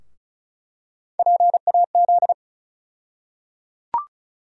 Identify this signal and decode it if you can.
Morse code, heard as a keyed tone, about 36 wpm, 700 Hz. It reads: PAZ
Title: USA Call Sign Postfixes - 3 Letters 35wpm